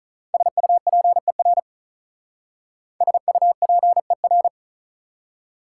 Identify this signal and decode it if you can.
Morse code, heard as a keyed tone, about 44 wpm, 700 Hz. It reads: SUPER SUPER